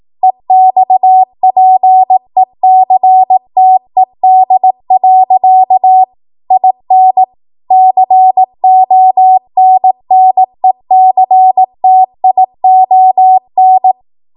Morse code, heard as a keyed tone, about 18 wpm, 750 Hz. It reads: EXPECTED. IN CONNECTION
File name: Morse / 18WPM